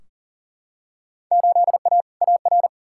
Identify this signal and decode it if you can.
Morse code, heard as a keyed tone, about 39 wpm, 700 Hz. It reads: 8A AR
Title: Field Day Exchanges Encore - 40wpm